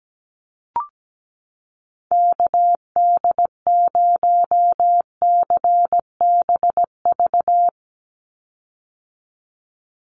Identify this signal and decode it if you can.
Morse code, heard as a keyed tone, about 17 wpm, 700 Hz. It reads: KD0CBV
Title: USA Call Signs 17wpm